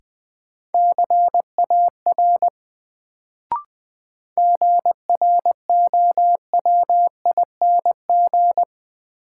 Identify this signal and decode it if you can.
Morse code, heard as a keyed tone, about 20 wpm, 700 Hz. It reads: CAR GROWING
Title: Sets of 5 Words Encore - New Word Drill - 20wpm